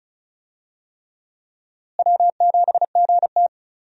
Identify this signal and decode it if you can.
Morse code, heard as a keyed tone, about 36 wpm, 700 Hz. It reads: W7GT
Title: USA Call Signs - 4 Characters - Repeated 3x - 35wpm